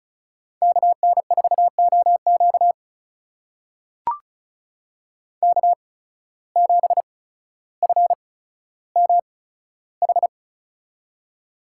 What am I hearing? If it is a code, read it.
Morse code, heard as a keyed tone, about 36 wpm, spaced as for 15 wpm, 700 Hz. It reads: KN4OQ K7FMH
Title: USA Call Signs - 5 Characters - 8x Character Spacing - 35wpm